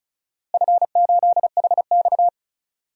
Morse code, heard as a keyed tone, about 36 wpm, 700 Hz. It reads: F8HX
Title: DXCC-100 Call Signs - 4 Characters - Repeated 2x - 35wpm